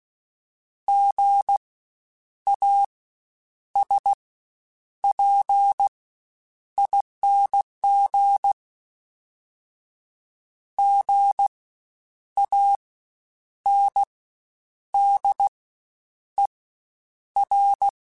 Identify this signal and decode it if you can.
Morse code, heard as a keyed tone, about 16 wpm, spaced as for 8 wpm, 800 Hz. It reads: GASPING GANDER
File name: Spaced Alphas / Spaced Alphas mp3s / Gs sp @15